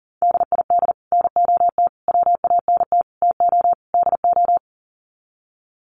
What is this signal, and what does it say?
Morse code, heard as a keyed tone, about 37 wpm, 700 Hz. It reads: DID NOT WANT TO DO
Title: Sets of 5 Words Encore 40wpm